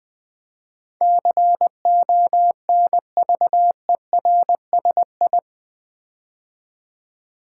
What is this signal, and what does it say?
Morse code, heard as a keyed tone, about 20 wpm, 700 Hz. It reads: CONVERSI
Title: Build A Word - 4 Letters At A Time - 10-Letter Words - 20wpm